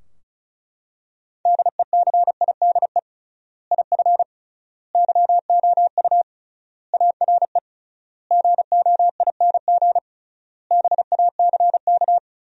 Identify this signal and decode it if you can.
Morse code, heard as a keyed tone, about 36 wpm, 700 Hz. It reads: DECIDE IF YOU ARE GOING BACK
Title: 3x Word-Spacing - Sentences from Top 500 Words - 35wpm